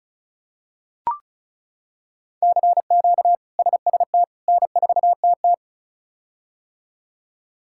Morse code, heard as a keyed tone, about 34 wpm, 700 Hz. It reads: CQ SST N4TT
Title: K1USN SST QSO Practice - 35wpm